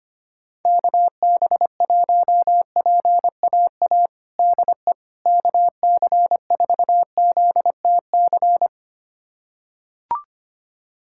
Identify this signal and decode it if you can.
Morse code, heard as a keyed tone, about 25 wpm, 700 Hz. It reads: KB1PAA DE KC4ZTC
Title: USA Callsign DE Callsign - 6 Characters - 25wpm